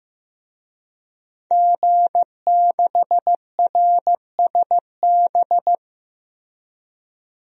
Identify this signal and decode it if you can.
Morse code, heard as a keyed tone, about 15 wpm, 700 Hz. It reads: G6RSB
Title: DXCC-100 Call Signs - 5 Characters - Repeated 3x - 15wpm